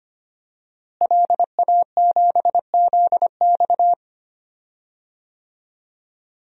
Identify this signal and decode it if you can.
Morse code, heard as a keyed tone, about 25 wpm, 700 Hz. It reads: LA7ZX
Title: DXCC-100 Call Signs - 5 Characters - Repeated 2x - 25wpm